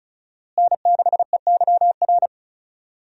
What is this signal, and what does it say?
Morse code, heard as a keyed tone, about 36 wpm, 700 Hz. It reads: N6EYR
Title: USA Call Signs - 5 Characters - Repeated 3x - 35wpm